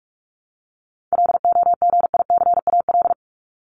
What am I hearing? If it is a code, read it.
Morse code, heard as a keyed tone, about 44 wpm, 700 Hz. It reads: LOGICAL
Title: Top 1501-1750 Words 45wpm